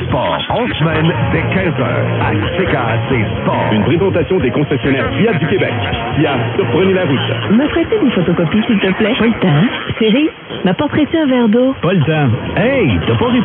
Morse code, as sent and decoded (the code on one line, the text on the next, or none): none